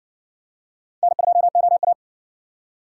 Morse code, heard as a keyed tone, about 53 wpm, 700 Hz. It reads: N2OA